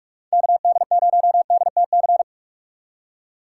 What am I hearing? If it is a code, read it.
Morse code, heard as a keyed tone, about 45 wpm, 700 Hz. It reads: KD0DTC
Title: USA Call Signs - 6 Characters - 6x Character Spacing - 45wpm